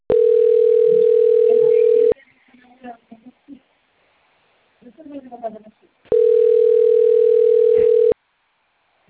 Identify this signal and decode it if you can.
Ringback tone (North America), 440 + 480 Hz, with speech.
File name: out- - - - - .95840